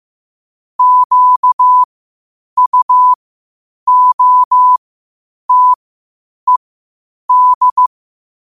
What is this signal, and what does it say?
Morse code, heard as a keyed tone, about 15 wpm, spaced as for 8 wpm, 1000 Hz. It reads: QUOTED
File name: qotdfiles / 8wpm